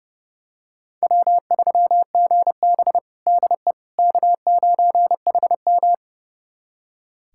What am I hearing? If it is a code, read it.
Morse code, heard as a keyed tone, about 32 wpm, 700 Hz. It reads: W3GB DE K9HM
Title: USA Callsign DE Callsign - 4 Characters - 30wpm